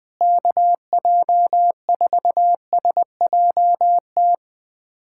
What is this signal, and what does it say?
Morse code, heard as a keyed tone, about 20 wpm, 700 Hz. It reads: KJ4SJT
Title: USA Callsign DE Callsign 20wpm